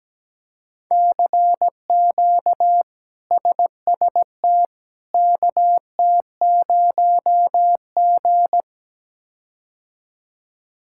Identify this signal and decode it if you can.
Morse code, heard as a keyed tone, about 17 wpm, 700 Hz. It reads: CQ SST KT0G
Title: K1USN SST QSO Practice - 17wpm